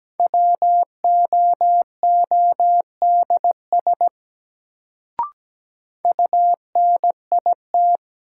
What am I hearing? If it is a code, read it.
Morse code, heard as a keyed tone, about 17 wpm, 700 Hz. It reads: WOODS UNIT
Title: Sentences from Top 1000 Words - Review Part 2 - New Word Drill - 17wpm